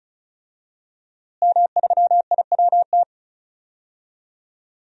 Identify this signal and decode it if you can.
Morse code, heard as a keyed tone, about 36 wpm, 700 Hz. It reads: M3IWT
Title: DXCC-100 Call Signs - 5 Characters - Repeated 3x - 35wpm